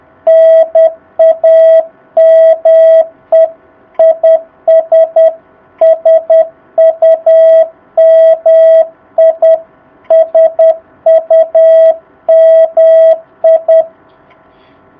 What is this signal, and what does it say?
Morse code, heard as a keyed tone, about 10 wpm, 660 Hz. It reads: NAME IS SUMI SUMI
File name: jh8cbh / 2019cw